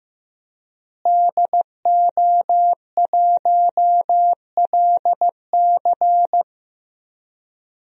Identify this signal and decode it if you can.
Morse code, heard as a keyed tone, about 15 wpm, 700 Hz. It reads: DO1LC
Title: DXCC-100 Call Signs - 5 Characters - Repeated 2x - 15wpm